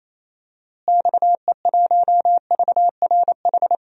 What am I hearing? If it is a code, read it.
Morse code, heard as a keyed tone, about 28 wpm, 700 Hz. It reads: XE1VRH
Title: DXCC-100 Call Signs - 6 Characters - Repeated 3x - 28wpm